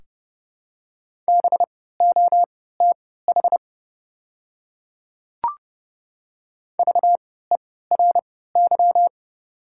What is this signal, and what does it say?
Morse code, heard as a keyed tone, about 30 wpm, spaced as for 17 wpm, 700 Hz. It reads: BOTH VERY
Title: 3x Character Spacing - Top 300 Words - 30wpm